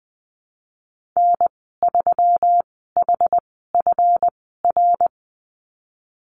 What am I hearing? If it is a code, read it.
Morse code, heard as a keyed tone, about 20 wpm, spaced as for 15 wpm, 700 Hz. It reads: N3HFR
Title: 161 Lesson 26g - Letter g - Call Signs 20wpm